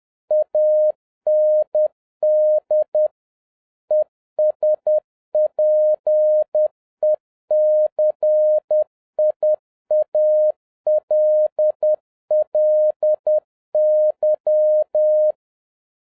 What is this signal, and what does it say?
Morse code, heard as a keyed tone, about 10 wpm, 610 Hz. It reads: AND ESPECIALLY
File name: 2011-07-mp3 / n31 THE 3.11 2 10wpm